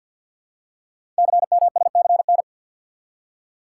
Morse code, heard as a keyed tone, about 51 wpm, 700 Hz. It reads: CMSCN
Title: Code Groups - 5 Letters - 50wpm